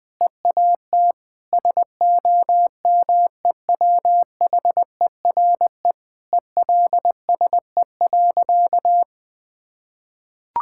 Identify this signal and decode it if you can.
Morse code, heard as a keyed tone, about 20 wpm, 700 Hz. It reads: EAT SOMEWHERE ELSE.
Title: Sentences with 5-Word Phrases - Spoken Beginning - 20wpm